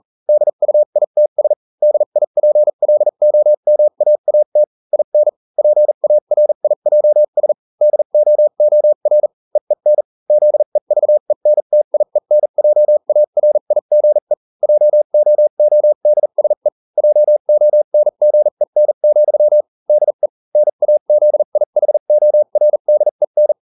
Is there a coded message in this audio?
Yes: Morse code from 0.3 to 23.5 s, heard as a keyed tone, about 40 wpm, spaced as for 35 wpm, 600 Hz. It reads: DUITS DIPLOMAAT IN PARIJS DOOR EEN ZEVENTIENJARIGE JOODSE JONGEN, DE NAZIHORDEN